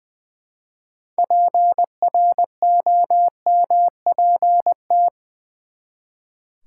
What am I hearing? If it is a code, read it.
Morse code, heard as a keyed tone, about 20 wpm, 700 Hz. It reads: PROMPT
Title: Six Letter Words - 20wpm